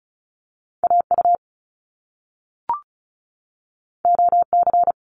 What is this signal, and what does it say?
Morse code, heard as a keyed tone, about 36 wpm, 700 Hz. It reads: AU OC
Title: Top Two Letter Combinations in Words 35wpm